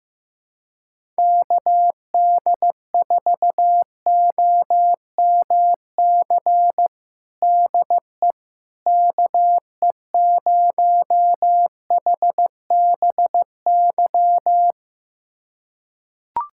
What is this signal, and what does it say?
Morse code, heard as a keyed tone, about 15 wpm, 700 Hz. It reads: KD4OMC DE KE0HBY
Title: USA Callsign DE Callsign - 6 Characters - 15wpm